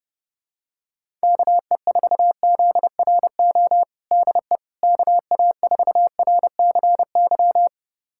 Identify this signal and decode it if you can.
Morse code, heard as a keyed tone, about 32 wpm, 700 Hz. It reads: KE4ZRO DE KA4RCY
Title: USA Callsign DE Callsign 30wpm